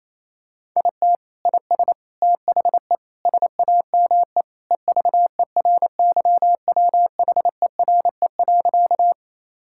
Morse code, heard as a keyed tone, about 27 wpm, 700 Hz. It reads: IT IS THE SAME EVERYWHERE.
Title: Sentences from Top 1600 Words 28wpm